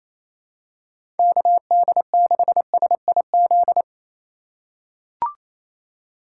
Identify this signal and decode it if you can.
Morse code, heard as a keyed tone, about 27 wpm, 700 Hz. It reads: KD6SIZ
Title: USA Call Signs - 6 Characters - 4x Character Spacing - 28wpm